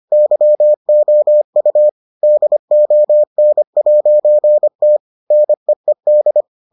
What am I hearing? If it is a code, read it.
Morse code, heard as a keyed tone, about 25 wpm, 600 Hz. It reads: YOU DON'T NEED